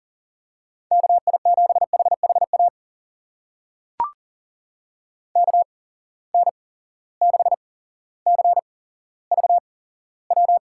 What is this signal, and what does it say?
Morse code, heard as a keyed tone, about 40 wpm, spaced as for 17 wpm, 700 Hz. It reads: KI7HHA KN6CVW
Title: USA Call Signs - 6 Characters - 8x Character Spacing - 40wpm